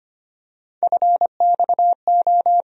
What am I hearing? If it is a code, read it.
Morse code, heard as a keyed tone, about 25 wpm, 700 Hz. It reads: FXO
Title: USA Call Sign Postfixes - 3 Letters 25wpm